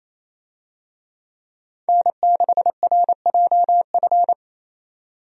Morse code, heard as a keyed tone, about 28 wpm, 700 Hz. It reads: N6RJF
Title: USA Call Signs - 5 Characters - 28wpm